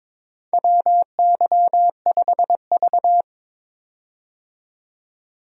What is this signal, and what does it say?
Morse code, heard as a keyed tone, about 22 wpm, 700 Hz. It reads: WY5V
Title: USA Call Signs - 4 Characters - 22wpm